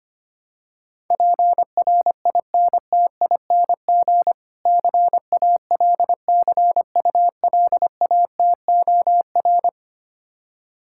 Morse code, heard as a keyed tone, about 25 wpm, 700 Hz. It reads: PRINTING CALCULATOR
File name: practice / 25wpm / Difficult Words - Speed-Racing with Hints - 25wpm